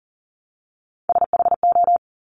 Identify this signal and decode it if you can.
Morse code, heard as a keyed tone, about 41 wpm, 700 Hz. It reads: SHO
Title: Top Three Letter Combinations in Words 40wpm